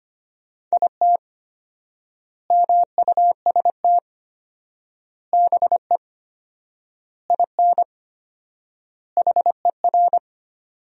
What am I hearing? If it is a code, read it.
Morse code, heard as a keyed tone, about 25 wpm, 700 Hz. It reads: IT MUST BE IN HER
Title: 4x Word-Spacing - Sentences from Top 300 Words - 25wpm